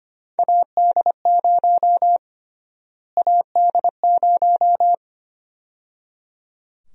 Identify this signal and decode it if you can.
Morse code, heard as a keyed tone, about 25 wpm, 700 Hz. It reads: AD0 AD0
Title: USA Call Sign Prefixes - Repeated 2x - 25wpm